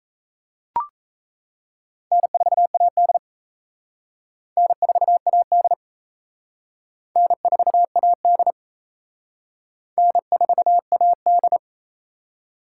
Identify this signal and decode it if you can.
Morse code, heard as a keyed tone, about 34 wpm, 700 Hz. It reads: N4AD N4AD N4AD N4AD
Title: Speed Racing - USA 1x2 Call Signs 28wpm